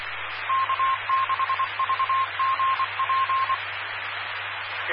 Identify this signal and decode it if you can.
Morse code, heard as a keyed tone, about 24 wpm, 1060 Hz, with radio noise. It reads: K6VGP
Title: Claremont, CA Police, Amateur Radio